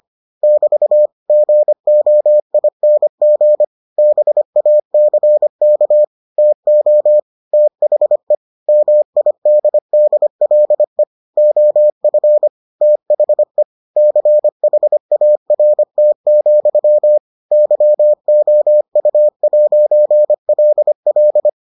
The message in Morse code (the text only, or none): <BT> GOING BACK TO THE MIDDLE OF THE CHART, YOU'LL